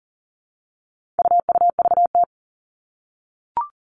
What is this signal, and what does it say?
Morse code, heard as a keyed tone, about 40 wpm, 700 Hz. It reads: UUVT
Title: All Cut Numbers - 4-Digit 40wpm